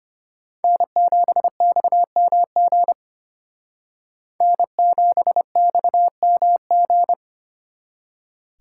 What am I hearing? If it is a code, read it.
Morse code, heard as a keyed tone, about 30 wpm, 700 Hz. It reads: N7XMG N7XMG
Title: Speed Racing - USA 1x3 Call Signs 25wpm